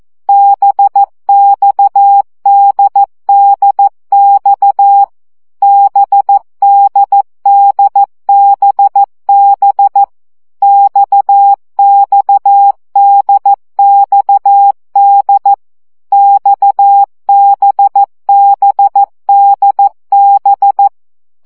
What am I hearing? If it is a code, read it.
Morse code, heard as a keyed tone, about 14 wpm, 800 Hz. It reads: BXDDX BDDBB XXDXD XBBDB
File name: CW gyakorlas d b x 12 baud